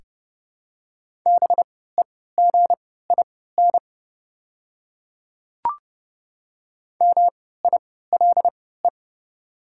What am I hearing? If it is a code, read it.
Morse code, heard as a keyed tone, about 28 wpm, spaced as for 17 wpm, 690 Hz. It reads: BEGIN MILE
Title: 3x Character Spacing - Top 500 Words - 30wpm